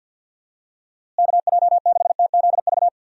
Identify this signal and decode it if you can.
Morse code, heard as a keyed tone, about 51 wpm, 700 Hz. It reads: KJ6TZV
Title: USA Call Signs - 6 Characters - Repeated 2x - 50wpm